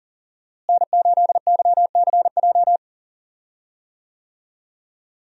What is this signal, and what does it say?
Morse code, heard as a keyed tone, about 40 wpm, 700 Hz. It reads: N8YCJ
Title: USA Call Signs - 5 Characters - Repeated 3x - 40wpm